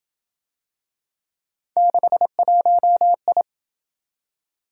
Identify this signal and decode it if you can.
Morse code, heard as a keyed tone, about 26 wpm, 700 Hz. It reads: 61I